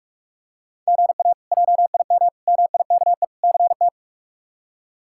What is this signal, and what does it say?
Morse code, heard as a keyed tone, about 46 wpm, 700 Hz. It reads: GA JIM MIKE CT
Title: K1USN SST QSO Practice - 45wpm